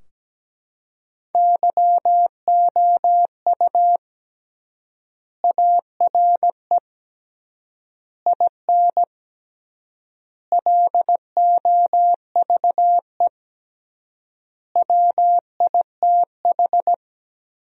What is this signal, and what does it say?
Morse code, heard as a keyed tone, about 17 wpm, 700 Hz. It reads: YOU ARE IN LOVE WITH
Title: 3x Word-Spacing - Sentences from Top 400 Words - 17wpm